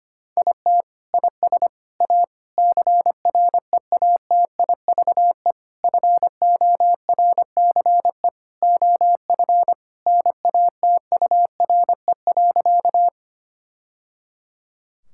Morse code, heard as a keyed tone, about 25 wpm, 700 Hz. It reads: IT IS A CREATIVE FORCE OF NATURE.
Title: Sentences from Top 1000 Words - Review Part 5 - 25wpm